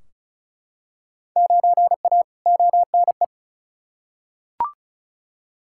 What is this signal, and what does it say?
Morse code, heard as a keyed tone, about 36 wpm, 700 Hz. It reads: 9A ONE
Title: Field Day Exchanges Encore - 35wpm